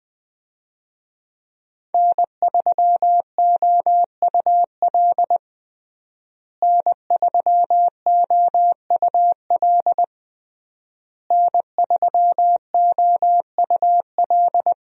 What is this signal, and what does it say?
Morse code, heard as a keyed tone, about 20 wpm, 700 Hz. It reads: N3OUL N3OUL N3OUL
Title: USA Call Signs - 5 Characters - Repeated 3x - 20wpm